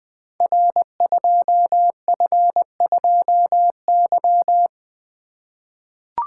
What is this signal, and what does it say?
Morse code, heard as a keyed tone, about 20 wpm, 700 Hz. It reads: R2F2Y
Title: Code Groups - 5 Letter:Numbers - 20wpm